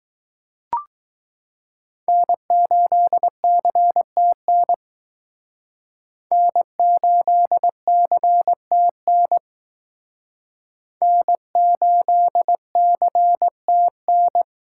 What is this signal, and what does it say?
Morse code, heard as a keyed tone, about 20 wpm, 700 Hz. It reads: N8CTN N8CTN N8CTN